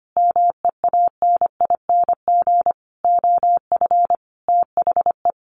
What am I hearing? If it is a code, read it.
Morse code, heard as a keyed tone, about 25 wpm, 700 Hz. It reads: MEANING OF THE
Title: Sentences from Top 900 Words 25wpm